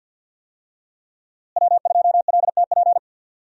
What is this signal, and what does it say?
Morse code, heard as a keyed tone, about 50 wpm, 700 Hz. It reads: W2LTP
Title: USA Call Signs - 5 Characters - Repeated 3x - 50wpm